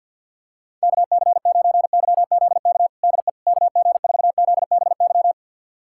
Morse code, heard as a keyed tone, about 50 wpm, 700 Hz. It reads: KC9YZK DE KG4ZBY
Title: USA Callsign DE Callsign 50wpm